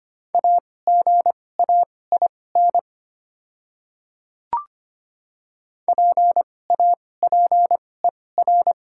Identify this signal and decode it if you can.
Morse code, heard as a keyed tone, about 25 wpm, spaced as for 18 wpm, 700 Hz. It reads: AGAIN PAPER
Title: 2x Character Spacing - Top 500 Words - 25wpm